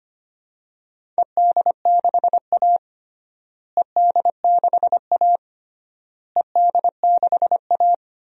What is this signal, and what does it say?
Morse code, heard as a keyed tone, about 25 wpm, 700 Hz. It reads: ED6A ED6A ED6A
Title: DXCC-100 Call Signs - 4 Characters - Repeated 3x - 25wpm